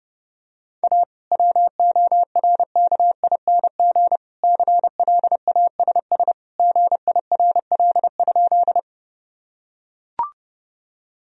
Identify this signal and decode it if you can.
Morse code, heard as a keyed tone, about 30 wpm, 700 Hz. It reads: A WORKING CLASS GIRL?
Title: Sentences from Top 1000 Words - Review Part 4 - 30wpm